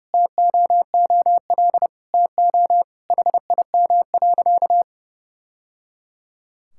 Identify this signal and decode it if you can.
Morse code, heard as a keyed tone, about 29 wpm, 700 Hz. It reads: TOOL TO HIM.
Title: Sentences from Top 1000 Words - Review Part 6 - 30wpm